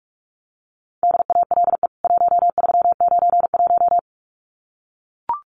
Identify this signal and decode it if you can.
Morse code, heard as a keyed tone, about 44 wpm, 700 Hz. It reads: DALE 1391
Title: CWT Exchange 45wpm